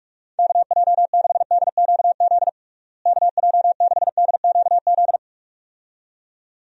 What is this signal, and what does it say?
Morse code, heard as a keyed tone, about 46 wpm, 700 Hz. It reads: KJ6DQZ KJ6DQZ
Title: USA Call Signs - 6 Characters - Repeated 2x - 45wpm